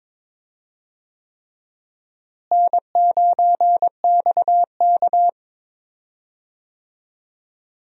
Morse code, heard as a keyed tone, about 22 wpm, 700 Hz. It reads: N9XK
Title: USA Call Signs - 4 Characters - Repeated 3x - 22wpm